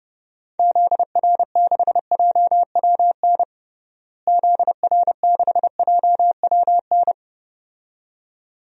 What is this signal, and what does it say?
Morse code, heard as a keyed tone, about 32 wpm, 700 Hz. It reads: ZR6JWN ZR6JWN
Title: DXCC-100 Call Signs - 6 Characters - Repeated 2x - 30wpm